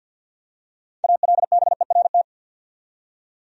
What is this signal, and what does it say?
Morse code, heard as a keyed tone, about 47 wpm, 700 Hz. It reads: ALBERT
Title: USA Common Names - 50wpm